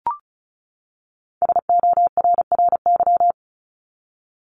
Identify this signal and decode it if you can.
Morse code, heard as a keyed tone, about 34 wpm, 700 Hz. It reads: SORRY
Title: Top 1751-2000 Words 35wpm